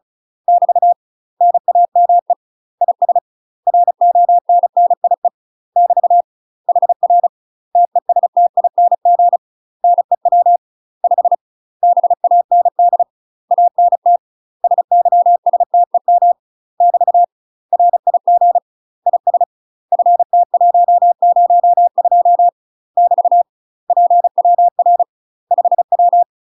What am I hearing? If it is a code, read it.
Morse code, heard as a keyed tone, about 36 wpm, 700 Hz. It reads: <BT> NAME IS RONNIE <BT> HR TESTING NEW 5 BAND ANT SYSTEM <BT> RIG IS FT102 <BT> PWR 5W
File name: UK MASTER QSOs TRAINING / 35WPM / UK MASTER QSOs 0031